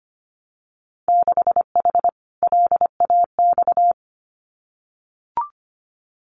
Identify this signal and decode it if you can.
Morse code, heard as a keyed tone, about 25 wpm, 700 Hz. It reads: 6H LAX
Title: Winter Field Day Exchanges Encore 25wpm